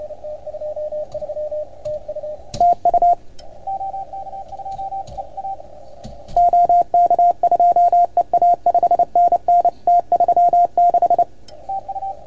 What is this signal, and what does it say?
Morse code, heard as a keyed tone, about 28 wpm, 690 Hz, with radio noise. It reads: TU OK2EA 5NN T36
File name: Archiv / CUC310 / OK1FGD CUC 310 pileup